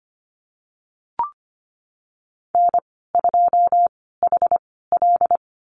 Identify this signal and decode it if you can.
Morse code, heard as a keyed tone, about 25 wpm, spaced as for 16 wpm, 700 Hz. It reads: N2HL
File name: Lesson 15g - Number 5 - Call Signs 25wpm